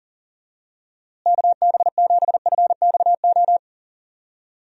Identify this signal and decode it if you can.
Morse code, heard as a keyed tone, about 40 wpm, 700 Hz. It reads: KB7FXO
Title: USA Call Signs - 6 Characters - Repeated 2x - 40wpm